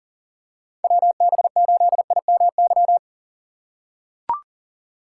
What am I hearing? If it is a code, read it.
Morse code, heard as a keyed tone, about 40 wpm, 700 Hz. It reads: WB8IMY
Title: K1USN SST QSO Practice - 40wpm